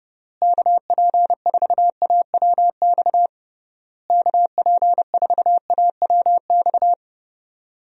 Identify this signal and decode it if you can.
Morse code, heard as a keyed tone, about 31 wpm, 700 Hz. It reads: KP4AWX KP4AWX
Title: DXCC-100 Call Signs - 6 Characters - Repeated 3x - 30wpm